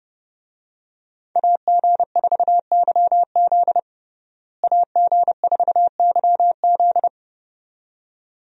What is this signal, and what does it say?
Morse code, heard as a keyed tone, about 30 wpm, 700 Hz. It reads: AG4YZ AG4YZ
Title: USA Call Signs - 5 Characters - Repeated 2x - 30wpm